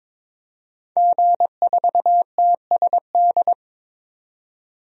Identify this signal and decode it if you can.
Morse code, heard as a keyed tone, about 22 wpm, 700 Hz. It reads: G4TSD
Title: DXCC-100 Call Signs - 5 Characters - 22wpm